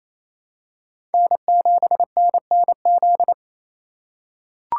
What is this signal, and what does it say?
Morse code, heard as a keyed tone, about 28 wpm, 700 Hz. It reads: N7NNZ
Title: USA Call Signs - 4x Character Spacing - 28wpm